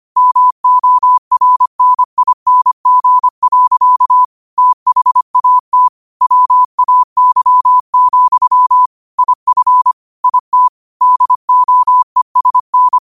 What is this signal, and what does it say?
Morse code, heard as a keyed tone, about 25 wpm, 1000 Hz. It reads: MORNING. THAT WAY, IF IT DOESN
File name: qotdfiles / 25wpm